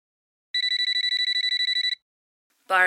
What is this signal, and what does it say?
There is no code.